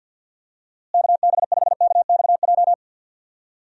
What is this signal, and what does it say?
Morse code, heard as a keyed tone, about 52 wpm, 700 Hz. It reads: KB5KXJ
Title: USA Call Signs - 6 Characters - Repeated 3x - 50wpm